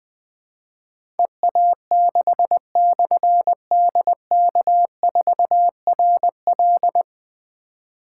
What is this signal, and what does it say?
Morse code, heard as a keyed tone, about 20 wpm, 700 Hz. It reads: EA6/DK4RL
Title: Difficult Call Signs 20wpm